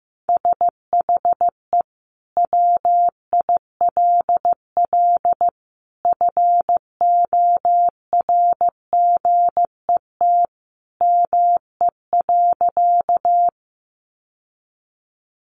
Morse code, heard as a keyed tone, about 15 wpm, 700 Hz. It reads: SHE WILL FORGET ME.
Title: Sentences from Top 1200 Words 15wpm